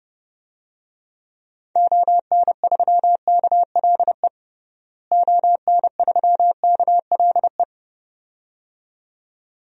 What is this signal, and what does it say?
Morse code, heard as a keyed tone, about 32 wpm, 700 Hz. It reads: ON3KLE ON3KLE
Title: DXCC-100 Call Signs - 6 Characters - Repeated 2x - 30wpm